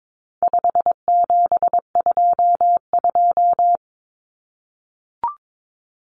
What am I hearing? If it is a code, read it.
Morse code, heard as a keyed tone, about 22 wpm, 700 Hz. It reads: 5722
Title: Common Cut Numbers - 4-Digit 22wpm